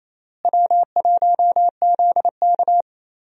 Morse code, heard as a keyed tone, about 28 wpm, 700 Hz. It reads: W1ZK
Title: USA Call Signs - 4 Characters - Repeated 3x - 28wpm